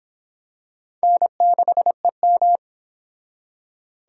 Morse code, heard as a keyed tone, about 25 wpm, 700 Hz. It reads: N6EM